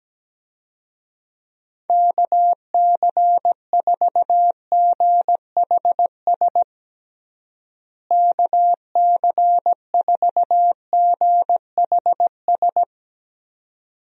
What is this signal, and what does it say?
Morse code, heard as a keyed tone, about 17 wpm, 700 Hz. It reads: KC4GHS KC4GHS
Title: USA Call Signs - 6 Characters - Repeated 2x - 17wpm